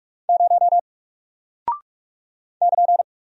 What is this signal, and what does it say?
Morse code, heard as a keyed tone, about 45 wpm, 700 Hz. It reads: 0 <KN>
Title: Single Character (Letter:Number:Punctuation:Prosign) Alt 45wpm